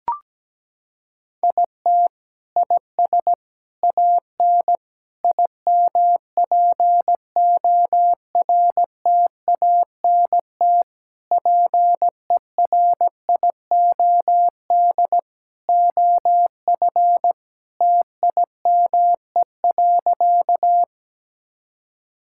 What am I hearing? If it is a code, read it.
Morse code, heard as a keyed tone, about 17 wpm, 700 Hz. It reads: IT IS AN IMPORTANT PERIOD OF TIME.
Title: Sentences from Top 1000 Words - Review Part 5 - 17wpm